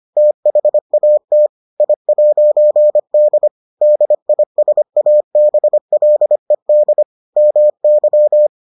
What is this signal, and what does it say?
Morse code, heard as a keyed tone, about 25 wpm, 600 Hz. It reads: THAT I'D DISABLED MY